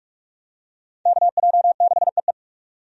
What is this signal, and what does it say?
Morse code, heard as a keyed tone, about 43 wpm, 700 Hz. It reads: KJ6EE
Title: USA Call Signs - 5 Characters - 45wpm